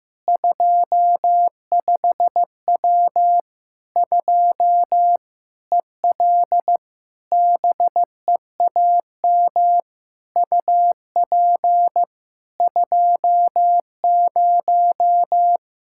Morse code, heard as a keyed tone, about 15 wpm, 700 Hz. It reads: 25W 2 EL BEAM UP 20
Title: Practice QSO - Exchange 2 - Info - 15wpm